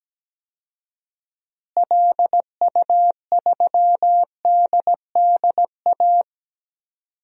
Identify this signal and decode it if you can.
Morse code, heard as a keyed tone, about 17 wpm, 700 Hz. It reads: LU3DDA
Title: DXCC-100 Call Signs - 6 Characters - Repeated 2x - 17wpm